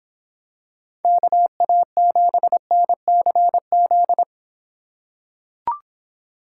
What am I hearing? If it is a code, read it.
Morse code, heard as a keyed tone, about 24 wpm, 700 Hz. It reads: KA7NCZ